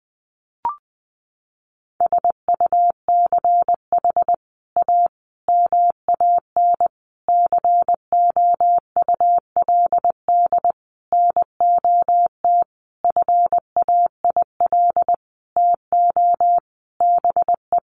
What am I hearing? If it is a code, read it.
Morse code, heard as a keyed tone, about 20 wpm, 700 Hz. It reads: SUCH A MAN COULD NOT FAIL TO BE
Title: Sentences from Top 1100 Words 20wpm